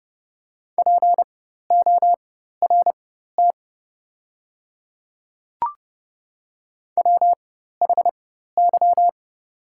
Morse code, heard as a keyed tone, about 30 wpm, spaced as for 14 wpm, 690 Hz. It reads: PORT WHY
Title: 4x Character Spacing - Top 200 Words - 30wpm